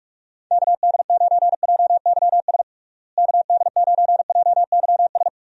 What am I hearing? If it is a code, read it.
Morse code, heard as a keyed tone, about 46 wpm, 700 Hz. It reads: KD9JYS KD9JYS
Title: USA Call Signs - 6 Characters - Repeated 3x - 45wpm